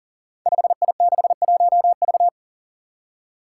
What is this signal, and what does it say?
Morse code, heard as a keyed tone, about 39 wpm, 700 Hz. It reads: 5I61V
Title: Five Challenging Letter-Number Combinations 40wpm